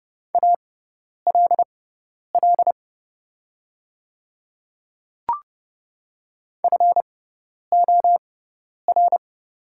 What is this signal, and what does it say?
Morse code, heard as a keyed tone, about 30 wpm, spaced as for 10 wpm, 700 Hz. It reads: ALL FOR
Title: 6x Character Spacing - Top 100 Words - 30wpm